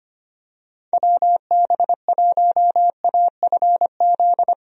Morse code, heard as a keyed tone, about 25 wpm, 700 Hz. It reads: WB1AFZ
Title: USA Call Signs - 6 Characters - Repeated 3x - 25wpm